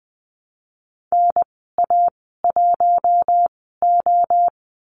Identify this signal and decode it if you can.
Morse code, heard as a keyed tone, about 20 wpm, spaced as for 15 wpm, 700 Hz. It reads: NA1O
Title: 076 Lesson 13g - Letter L - Call Signs 20wpm